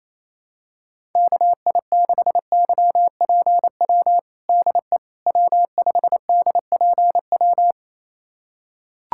Morse code, heard as a keyed tone, about 27 wpm, 700 Hz. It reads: KI6YPW DE W5DPW
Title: USA Callsign DE Callsign 28wpm